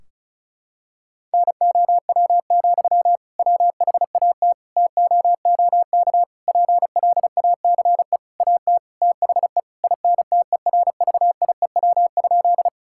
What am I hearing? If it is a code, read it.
Morse code, heard as a keyed tone, about 34 wpm, 700 Hz. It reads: NOW, WHAT TOOK PLACE AT THE INTERVIEW?
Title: Sentences from Top 1600 Words 35wpm